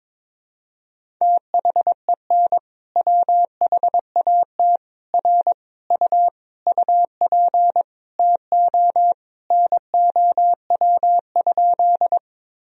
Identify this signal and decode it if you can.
Morse code, heard as a keyed tone, about 22 wpm, 700 Hz. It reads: THEN WHAT R U UP TO NOW?
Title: Sentences from Top 100 Words and Abbreviations 22wpm